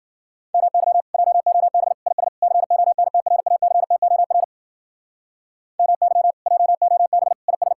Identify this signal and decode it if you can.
Morse code, heard as a keyed tone, about 48 wpm, 700 Hz. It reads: MY JOB IS CONTRACTOR MY JOB IS
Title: Speed-Racing Common QSO Phrases - 40wpm